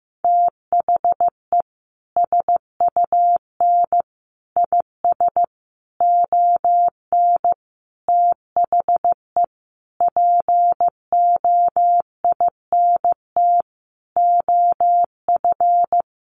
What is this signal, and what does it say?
Morse code, heard as a keyed tone, about 15 wpm, 700 Hz. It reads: THE SUN IS ON THE POINT OF
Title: Sentences from Top 1300 Words 15wpm